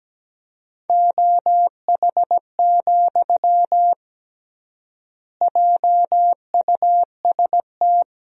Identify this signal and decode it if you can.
Morse code, heard as a keyed tone, about 17 wpm, 700 Hz. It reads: OH, JUST
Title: 3x Word-Spacing - Sentences from Top 500 Words - 17wpm